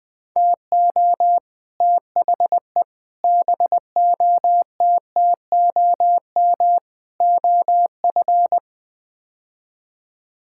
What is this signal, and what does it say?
Morse code, heard as a keyed tone, about 20 wpm, 700 Hz. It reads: TO THE BOTTOM OF
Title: Build A Phrase - 5-Word Phrases - 20wpm